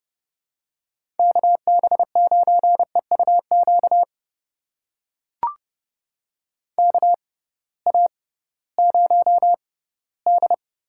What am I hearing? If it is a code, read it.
Morse code, heard as a keyed tone, about 30 wpm, spaced as for 18 wpm, 700 Hz. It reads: KB9EUQ KA0D
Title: USA Call Signs - 6 Characters - 6x Character Spacing - 30wpm